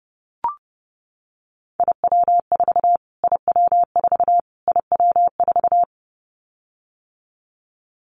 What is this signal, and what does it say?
Morse code, heard as a keyed tone, about 30 wpm, 690 Hz. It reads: IW4 IW4 IW4
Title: Callsign Prefixes for Top 25 DXCC - Repeated 3x - 30wpm